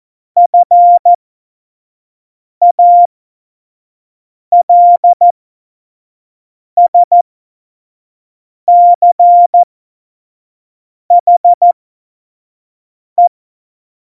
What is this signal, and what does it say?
Morse code, heard as a keyed tone, about 14 wpm, spaced as for 5 wpm, 700 Hz. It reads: FALSCHE
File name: Morsekurs / Witze / Witz 12. Januar 2021, 25 F70